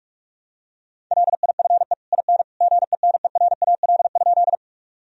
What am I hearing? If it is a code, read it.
Morse code, heard as a keyed tone, about 47 wpm, 700 Hz. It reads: LIFE IN GENERAL?
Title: Sentences from Top 1000 Words - Review Part 3 - 45wpm